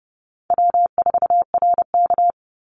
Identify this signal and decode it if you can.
Morse code, heard as a keyed tone, about 30 wpm, 700 Hz. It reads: W4RK
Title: Easy Call Signs - 30wpm